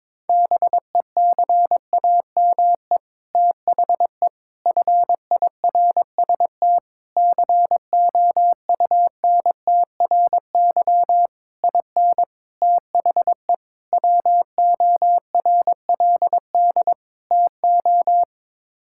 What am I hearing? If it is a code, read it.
Morse code, heard as a keyed tone, about 22 wpm, 700 Hz. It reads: BECAME THE FIRST COUNTRY IN THE WORLD TO
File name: practice / 22wpm / Sets of 8 Words - 22wpm